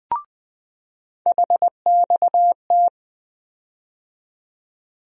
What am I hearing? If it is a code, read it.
Morse code, heard as a keyed tone, about 20 wpm, 700 Hz. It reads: HXT
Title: USA Call Sign Postfixes - 3 Letters 20wpm